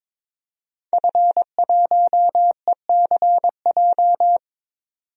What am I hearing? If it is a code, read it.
Morse code, heard as a keyed tone, about 22 wpm, 700 Hz. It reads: F1ECJ
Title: DXCC-100 Call Signs - 5 Characters - 22wpm